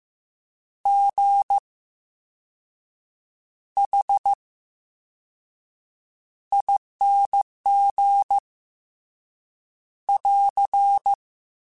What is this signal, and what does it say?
Morse code, heard as a keyed tone, about 15 wpm, 800 Hz. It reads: G H I N G <AR>
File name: Special INGs-mp3s / Special INGs-mp3s / VerySlowToFaster Ing @13